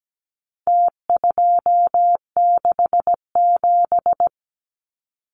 Morse code, heard as a keyed tone, about 17 wpm, 700 Hz. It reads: T267
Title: Numbers - Contest Style - 4-Digit 17wpm